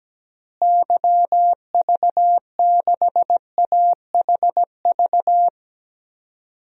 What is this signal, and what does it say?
Morse code, heard as a keyed tone, about 17 wpm, 700 Hz. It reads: YV6AHV
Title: DXCC-100 Call Signs - 6 Characters - Repeated 2x - 17wpm